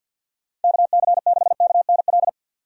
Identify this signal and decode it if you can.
Morse code, heard as a keyed tone, about 49 wpm, 700 Hz. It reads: KC6KNL
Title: USA Call Signs - 6 Characters - 50wpm